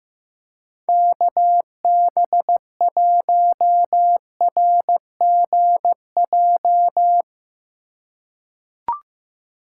Morse code, heard as a keyed tone, about 15 wpm, 700 Hz. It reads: KB1RGJ
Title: USA Call Signs - 6 Characters - 4x Character Spacing - 15wpm